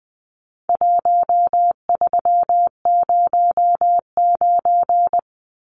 Morse code, heard as a keyed tone, about 20 wpm, 700 Hz. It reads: 1309
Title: Numbers - 4-Digit 20wpm